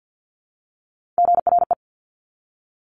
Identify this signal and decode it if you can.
Morse code, heard as a keyed tone, about 50 wpm, 700 Hz. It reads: GRE
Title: Top Three Letter Combinations in Words 50wpm